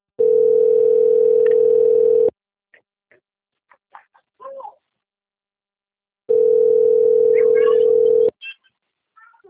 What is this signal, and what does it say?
Ringback tone (North America), 440 + 480 Hz.
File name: out- - - - - .11370